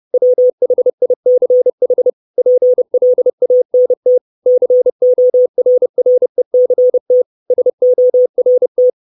Morse code, heard as a keyed tone, about 30 wpm, 500 Hz. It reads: WHICH PLANT CORRECT SORT
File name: 632 Words 30WPM